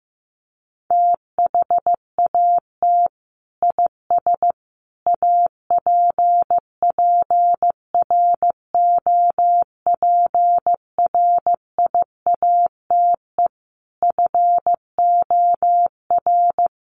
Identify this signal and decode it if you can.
Morse code, heard as a keyed tone, about 15 wpm, 700 Hz. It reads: THAT IS APPROPRIATE FOR
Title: Sentences from Top 900 Words 15wpm